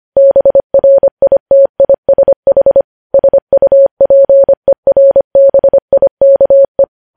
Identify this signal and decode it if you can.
Morse code, heard as a keyed tone, about 25 wpm, 550 Hz. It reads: BRITISH SUPERBIKE
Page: news headlines in morse code at 25 wpm